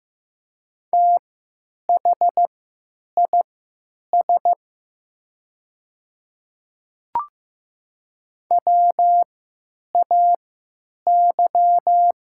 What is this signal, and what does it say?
Morse code, heard as a keyed tone, about 15 wpm, spaced as for 9 wpm, 700 Hz. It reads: THIS WAY
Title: 3x Character Spacing - Top 100 Words - 15wpm